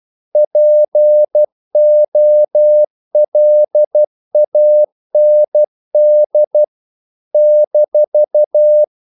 Morse code, heard as a keyed tone, about 12 wpm, 600 Hz. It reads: POLAND -
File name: CW / Practice / NationalCapitals12